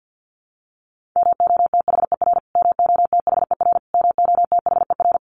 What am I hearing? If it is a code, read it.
Morse code, heard as a keyed tone, about 48 wpm, 700 Hz. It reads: MOTHER MOTHER MOTHER
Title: Top 200 Words Repeated 50wpm